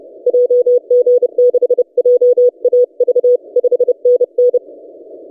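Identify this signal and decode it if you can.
Morse code, heard as a keyed tone, about 30 wpm, 490 Hz. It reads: JG6JAV 5NN
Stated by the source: DX station 3D2GC on 20 metres CW, recorded by N6ML.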